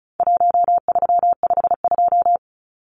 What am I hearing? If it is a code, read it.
Morse code, heard as a keyed tone, about 34 wpm, 700 Hz. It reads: 1352
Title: Numbers - Contest Style - 4-Digit 35wpm